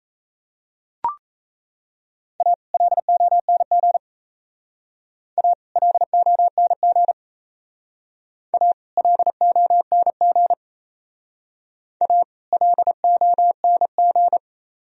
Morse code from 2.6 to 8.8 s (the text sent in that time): LONG A LONG A